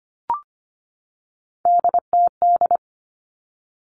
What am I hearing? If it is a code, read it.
Morse code, heard as a keyed tone, about 25 wpm, 700 Hz. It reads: DTD
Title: Three Challenging Letter-Number Combinations 25wpm